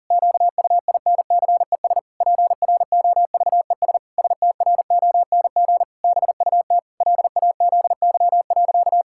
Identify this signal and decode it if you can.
Morse code, heard as a keyed tone, about 40 wpm, 700 Hz. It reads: QUINCES PROVES STRONG BUT LAZY.
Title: Pangram Challenge v2 - 40wpm